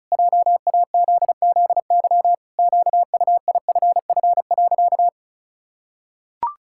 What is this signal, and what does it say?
Morse code, heard as a keyed tone, about 36 wpm, 700 Hz. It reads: JAZZY QUIFF.
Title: Pangram Challenge v2 - 35wpm